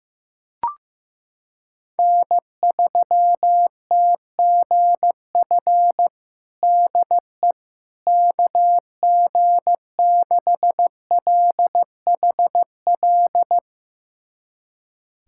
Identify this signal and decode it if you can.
Morse code, heard as a keyed tone, about 15 wpm, 700 Hz. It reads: N3TGF DE KG6LHL
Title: USA Callsign DE Callsign 15wpm